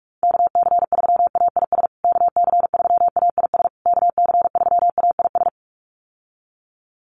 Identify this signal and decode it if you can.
Morse code, heard as a keyed tone, about 44 wpm, 700 Hz. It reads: KC3AIS KC3AIS KC3AIS
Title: USA Call Signs Repeated 3x 45wpm